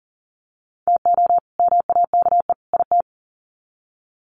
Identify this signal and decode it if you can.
Morse code, heard as a keyed tone, about 40 wpm, 700 Hz. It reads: TO MAKE IT
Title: Sets of 3 Words 40wpm